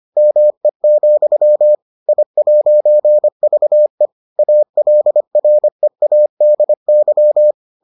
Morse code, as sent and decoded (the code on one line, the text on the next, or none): -- . --..-- / .. .----. ...- . / .- .-.. .-. .- -.. -.--
ME, I'VE ALREADY